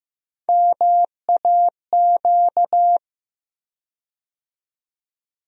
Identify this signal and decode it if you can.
Morse code, heard as a keyed tone, about 15 wpm, 700 Hz. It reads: MAQ
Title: USA Call Sign Postfixes - 3 Letters 15wpm